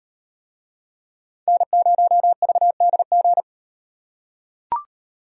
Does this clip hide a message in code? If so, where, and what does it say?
Yes: Morse code from 1.5 to 3.4 s, heard as a keyed tone, about 38 wpm, 700 Hz. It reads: N0VDG